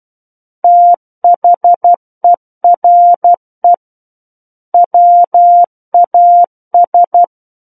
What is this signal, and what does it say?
Morse code, heard as a keyed tone, about 12 wpm, 700 Hz. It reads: THERE WAS
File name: Book One / 03-Chapter 3 - 12 WPM